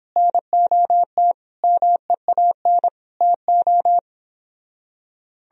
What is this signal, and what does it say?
Morse code, heard as a keyed tone, about 29 wpm, 700 Hz. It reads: NOT MEAN TO